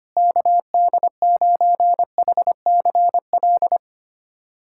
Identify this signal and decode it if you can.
Morse code, heard as a keyed tone, about 25 wpm, 700 Hz. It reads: KD9HCL
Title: USA Callsign DE Callsign - 6 Characters - 25wpm